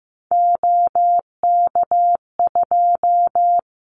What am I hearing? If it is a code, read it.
Morse code, heard as a keyed tone, about 15 wpm, 700 Hz. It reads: OK2
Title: Callsign Prefixes for Top 50 DXCC - 15wpm